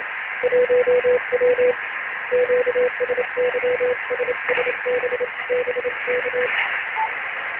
Morse code, heard as a keyed tone, about 28 wpm, 500 Hz. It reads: 1W QSY SSB BK
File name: ARCHIV / BPSK CW / OK1IF-OK1ANG- -1230CET-3766.5-3767